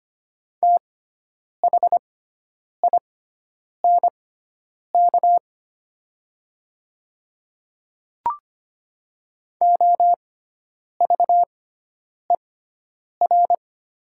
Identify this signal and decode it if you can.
Morse code, heard as a keyed tone, about 25 wpm, spaced as for 9 wpm, 700 Hz. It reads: THINK OVER
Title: 6x Character Spacing - Top 200 Words - 25wpm